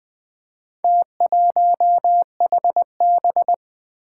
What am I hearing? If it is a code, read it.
Morse code, heard as a keyed tone, about 20 wpm, 700 Hz. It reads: T1HB
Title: Four Challenging Letter-Number Combinations 20wpm